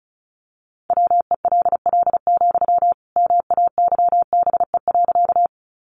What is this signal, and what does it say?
Morse code, heard as a keyed tone, about 34 wpm, 700 Hz. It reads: WELL, MAYBE.
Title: Sentences from Top 1000 Words 35wpm